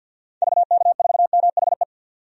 Morse code, heard as a keyed tone, about 52 wpm, 700 Hz. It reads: VK4MHE
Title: DXCC-100 Call Signs - 6 Characters - 50wpm